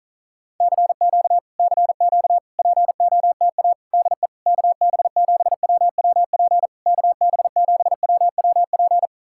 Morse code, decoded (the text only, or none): CQ CQ POTA DE KB7WWP KB7WWP